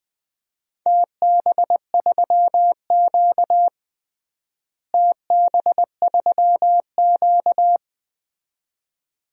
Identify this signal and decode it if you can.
Morse code, heard as a keyed tone, about 20 wpm, 700 Hz. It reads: TB3Q TB3Q
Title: DXCC-100 Call Signs - 4 Characters - Repeated 2x - 20wpm